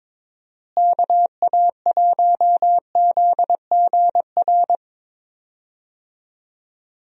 Morse code, heard as a keyed tone, about 22 wpm, 700 Hz. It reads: KA1ZGR
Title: USA Call Signs - 6 Characters - Repeated 3x - 22wpm